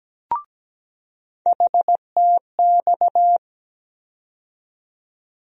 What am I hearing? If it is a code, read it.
Morse code, heard as a keyed tone, about 17 wpm, 700 Hz. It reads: HTX
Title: USA Call Sign Postfixes - 3 Letters 17wpm